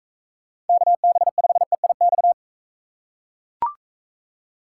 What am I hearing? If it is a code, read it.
Morse code, heard as a keyed tone, about 42 wpm, 700 Hz. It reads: KB5EIX